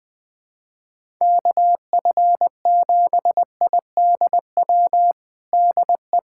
Morse code, heard as a keyed tone, about 20 wpm, 700 Hz. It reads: KF7IDW DE
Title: USA Callsign DE Callsign - 6 Characters - 20wpm